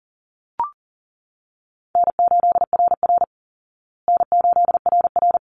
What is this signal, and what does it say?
Morse code, heard as a keyed tone, about 40 wpm, 700 Hz. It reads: N8RR N8RR
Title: Easy Call Signs Repeated 2x - 40wpm